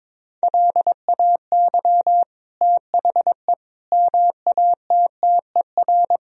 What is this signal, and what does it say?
Morse code, heard as a keyed tone, about 22 wpm, 700 Hz. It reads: LAY THE MATTER
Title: Sentences from Top 1000 Words - Review Part 4 - 22wpm